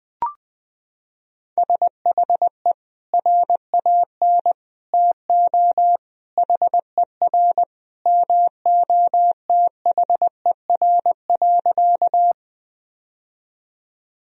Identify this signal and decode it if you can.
Morse code, heard as a keyed tone, about 20 wpm, 700 Hz. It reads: SHE RAN TO HER MOTHER.
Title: Sentences from Top 1000 Words - Review Part 3 - 20wpm